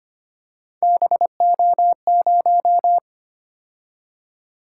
Morse code, heard as a keyed tone, about 25 wpm, 700 Hz. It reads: BO0
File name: Three Letter-Number Combinations - 25wpm